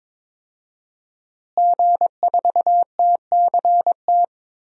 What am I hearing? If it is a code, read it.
Morse code, heard as a keyed tone, about 23 wpm, 700 Hz. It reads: G4TCT
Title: DXCC-100 Call Signs - 5 Characters - Repeated 2x - 22wpm